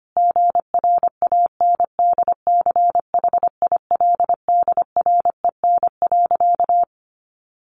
Morse code, heard as a keyed tone, about 25 wpm, 700 Hz. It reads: GRANDCHILDREN.
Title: A Sound Of Thunder 25wpm